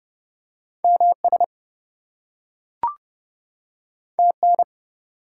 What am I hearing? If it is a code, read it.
Morse code, heard as a keyed tone, about 30 wpm, 700 Hz. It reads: MS TN
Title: US States and Canadian Provinces 30wpm